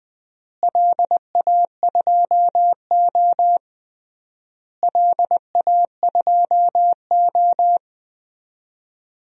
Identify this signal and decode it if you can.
Morse code, heard as a keyed tone, about 20 wpm, 700 Hz. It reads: LA2O LA2O
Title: DXCC-100 Call Signs - 4 Characters - Repeated 3x - 20wpm